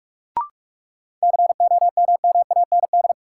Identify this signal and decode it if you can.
Morse code, heard as a keyed tone, about 46 wpm, 700 Hz. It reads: COMMAND
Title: Sentences from Top 1600 Words - New Word Drill - 45wpm